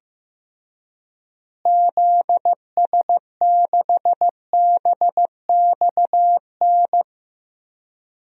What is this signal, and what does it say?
Morse code, heard as a keyed tone, about 15 wpm, 700 Hz. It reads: ZS6BXN
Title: DXCC-100 Call Signs - 6 Characters - Repeated 3x - 15wpm